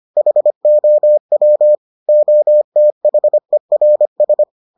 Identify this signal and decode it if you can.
Morse code, heard as a keyed tone, about 25 wpm, 600 Hz. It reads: HOW OTHERS